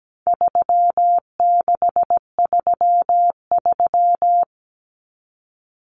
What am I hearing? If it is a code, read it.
Morse code, heard as a keyed tone, about 17 wpm, 700 Hz. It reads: 3633
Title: Numbers - 4-Digit 17wpm